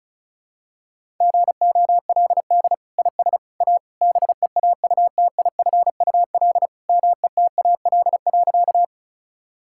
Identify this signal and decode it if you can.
Morse code, heard as a keyed tone, about 36 wpm, 700 Hz. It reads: GOLD IS A BEAUTIFUL METAL.
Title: Sentences from Top 1000 Words - Review Part 6 - 35wpm